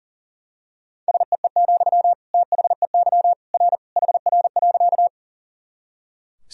Morse code, heard as a keyed tone, about 37 wpm, 700 Hz. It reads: SEE, THEY R HR.
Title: Sentences from Top 200 Words and Abbreviations 40wpm